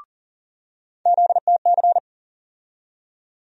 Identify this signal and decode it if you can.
Morse code, heard as a keyed tone, about 38 wpm, 700 Hz. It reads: ZTC
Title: USA Call Sign Postfixes - 3 Letters 40wpm